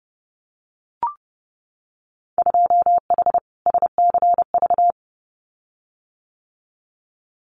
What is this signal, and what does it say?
Morse code, heard as a keyed tone, about 31 wpm, 700 Hz. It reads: 2H SCV
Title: Winter Field Day Exchanges Encore 30wpm